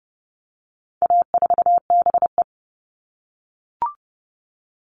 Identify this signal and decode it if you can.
Morse code, heard as a keyed tone, about 30 wpm, 700 Hz. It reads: A4BE
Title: All Cut Numbers - 4-Digit 30wpm